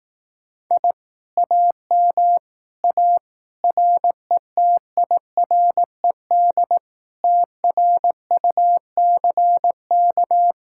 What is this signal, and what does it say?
Morse code, heard as a keyed tone, about 18 wpm, 700 Hz. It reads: I AM A RETIRED TRUCK
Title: Speed-Racing Common QSO Phrases Encore - 15wpm